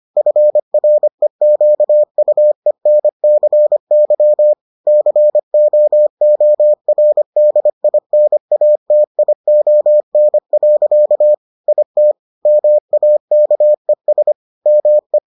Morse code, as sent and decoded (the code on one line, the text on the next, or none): ..-. .-. --.- ..- . -. -.-. -.-- / -.-. --- --- .-. -.. .. -. .- - .. --- -. .-.-.- / .. - / -- .- -.- . ... / -- .
FREQUENCY COORDINATION. IT MAKES ME